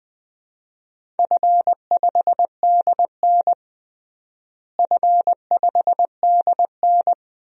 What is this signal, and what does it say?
Morse code, heard as a keyed tone, about 21 wpm, 700 Hz. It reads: F5DN F5DN
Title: DXCC-100 Call Signs - 4 Characters - Repeated 2x - 20wpm